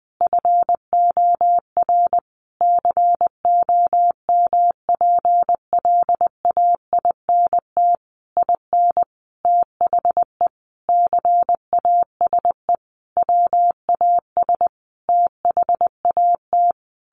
Morse code, heard as a keyed tone, about 19 wpm, 700 Hz. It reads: FOR COMPLAINT IN THE CASE WAS THAT
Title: Sentences from Top 1400 Words 20wpm